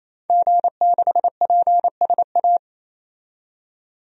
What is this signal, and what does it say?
Morse code, heard as a keyed tone, about 28 wpm, 700 Hz. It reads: G6PSA
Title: DXCC-100 Call Signs - 5 Characters - 28wpm